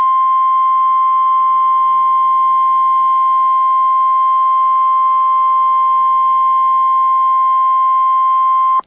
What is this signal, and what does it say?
Emergency Alert System (SAME) tones (the 1050 Hz attention tone). It is heard from 1.1 to 8.5 s.